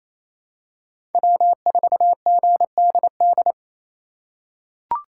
Morse code, heard as a keyed tone, about 28 wpm, 700 Hz. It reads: W4GDD
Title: USA Call Signs - 5 Characters - 2x Character Spacing - 28wpm